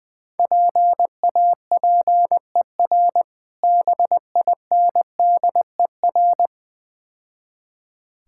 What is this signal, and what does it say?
Morse code, heard as a keyed tone, about 20 wpm, 700 Hz. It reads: PAPER BINDER
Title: Difficult Words - Speed-Racing with Hints - 20wpm